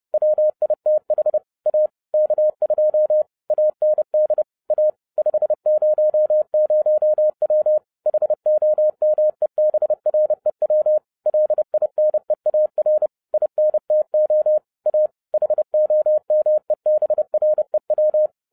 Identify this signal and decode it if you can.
Morse code, heard as a keyed tone, about 30 wpm, 610 Hz. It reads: WITH A K2 AND A 500W HOMEBREW LINEAR INTO A HOMEBREW
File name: 2011-07-mp3 / n20 GREETINGS FROM SINGAPORE 30wpm